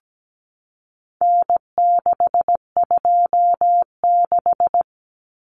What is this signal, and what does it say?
Morse code, heard as a keyed tone, about 17 wpm, 700 Hz. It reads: N626
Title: Common Cut Numbers - 4-Digit 17wpm